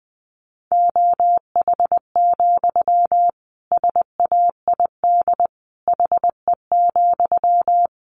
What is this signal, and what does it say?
Morse code, heard as a keyed tone, about 20 wpm, 700 Hz. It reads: OH, SAID HE,
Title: Sentences from Top 500 Words 20wpm